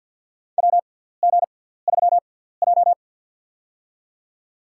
Morse code, heard as a keyed tone, about 53 wpm, spaced as for 17 wpm, 700 Hz. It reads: WG3J
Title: USA Call Signs - 6x Character Spacing - 50wpm